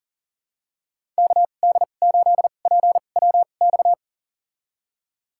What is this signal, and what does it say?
Morse code, heard as a keyed tone, about 40 wpm, spaced as for 29 wpm, 700 Hz. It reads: KD8PWX
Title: USA Call Signs - 2x Character Spacing - 40wpm